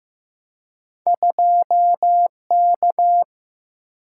Morse code, heard as a keyed tone, about 15 wpm, 700 Hz. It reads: 2K
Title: Single-Letter Single-Number 15wpm